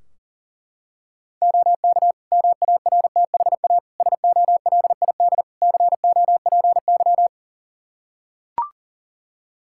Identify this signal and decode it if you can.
Morse code, heard as a keyed tone, about 40 wpm, 700 Hz. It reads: OK MARTHA SOLID COPY
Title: Practice QSO - Exchange 3 - More Info - 40wpm